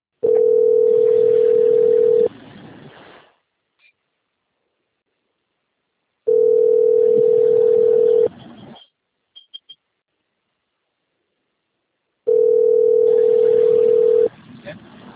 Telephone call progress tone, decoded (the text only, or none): ringback tone